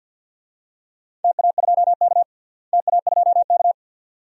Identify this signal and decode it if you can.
Morse code, heard as a keyed tone, about 52 wpm, 700 Hz. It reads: TA2K TA2K
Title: DXCC-100 Call Signs - 4 Characters - Repeated 2x - 50wpm